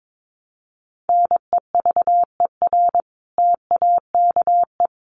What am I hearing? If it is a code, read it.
Morse code, heard as a keyed tone, about 22 wpm, 700 Hz. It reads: NEVER TAKE
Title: Sentences from Top 1100 Words 22wpm